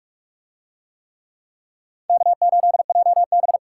Morse code, heard as a keyed tone, about 44 wpm, 700 Hz. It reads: K8JB